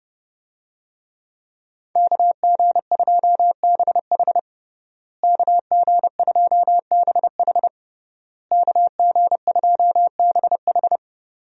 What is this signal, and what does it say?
Morse code, heard as a keyed tone, about 30 wpm, 700 Hz. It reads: KG2BH KG2BH KG2BH
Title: USA Call Signs - 5 Characters - Repeated 3x - 30wpm